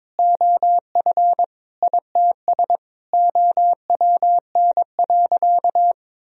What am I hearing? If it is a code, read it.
Morse code, heard as a keyed tone, about 22 wpm, 700 Hz. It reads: OF ITS OWN.
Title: Sentences from Top 1000 Words - Review Part 4 - 22wpm